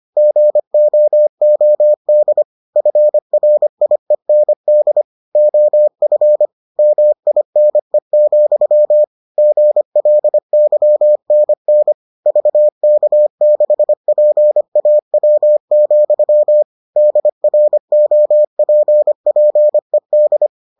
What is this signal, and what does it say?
Morse code, heard as a keyed tone, about 25 wpm, 600 Hz. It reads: GOOD FRIEND OF MINE, GLYNN VK6PAW, DROPPED